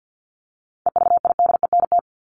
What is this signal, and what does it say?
Morse code, heard as a keyed tone, about 48 wpm, 700 Hz. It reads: EVIDENT
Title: LY Suffix - Top 100 Words with Base Words 50wpm